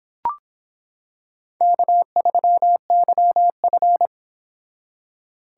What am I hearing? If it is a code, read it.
Morse code, heard as a keyed tone, about 25 wpm, 700 Hz. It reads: K3YF